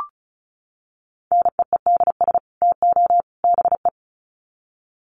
Morse code, heard as a keyed tone, about 34 wpm, 700 Hz. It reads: NEEDS TO BE
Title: Sets of 3 Words 35wpm